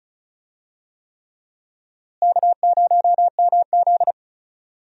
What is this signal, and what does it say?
Morse code, heard as a keyed tone, about 34 wpm, 700 Hz. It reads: K0MZ
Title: USA Call Signs - 4 Characters - Repeated 3x - 35wpm